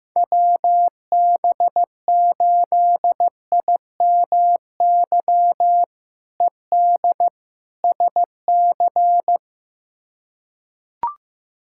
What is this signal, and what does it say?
Morse code, heard as a keyed tone, about 16 wpm, 700 Hz. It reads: WB8IMY ED SC
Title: K1USN SST QSO Practice - 15wpm